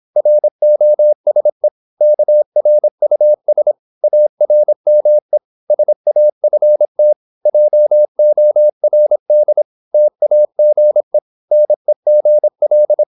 Morse code, heard as a keyed tone, about 26 wpm, 600 Hz. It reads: ROSE KRUS ARME SAFT JORD TAGE NEGL
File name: Korte ord DK 26wpm